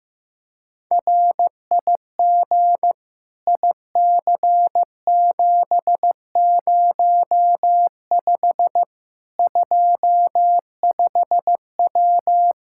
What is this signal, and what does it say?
Morse code, heard as a keyed tone, about 15 wpm, 700 Hz. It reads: RIG IC705 25W
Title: Practice QSO - Exchange 2 - Info - 15wpm